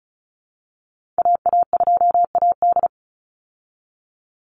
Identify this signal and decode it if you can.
Morse code, heard as a keyed tone, about 34 wpm, 700 Hz. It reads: AA2AD
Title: Call Signs Repeated 2x - 35wpm